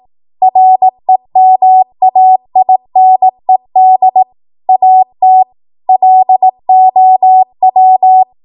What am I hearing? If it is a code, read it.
Morse code, heard as a keyed tone, about 18 wpm, 750 Hz. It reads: REMAINED AT LOW
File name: Morse / 18WPM